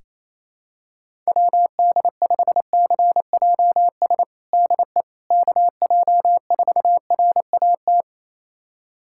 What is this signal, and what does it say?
Morse code, heard as a keyed tone, about 28 wpm, 700 Hz. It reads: WD5CJS DE KJ4RAT
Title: USA Callsign DE Callsign 28wpm